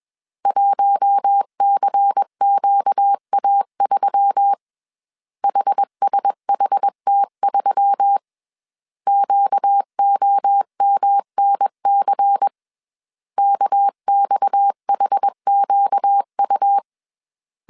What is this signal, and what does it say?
Morse code, heard as a keyed tone, about 21 wpm, 800 Hz. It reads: 1CQA3 HSHT3 QOMNC KXHQU
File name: CW / DET / 1 6 1500C S A 20wpm 2048